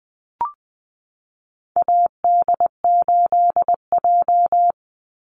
Morse code, heard as a keyed tone, about 20 wpm, 700 Hz. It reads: AD8J
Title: Call Signs - 20wpm